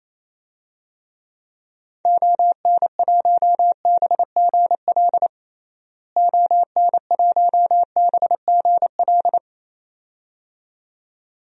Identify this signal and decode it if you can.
Morse code, heard as a keyed tone, about 28 wpm, 700 Hz. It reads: ON1BGL ON1BGL
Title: DXCC-100 Call Signs - 6 Characters - Repeated 2x - 28wpm